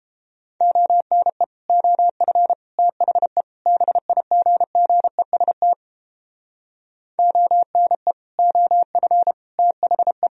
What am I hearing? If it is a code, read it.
Morse code, heard as a keyed tone, about 29 wpm, 700 Hz. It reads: ONE OF THE BIGGEST ONE OF THE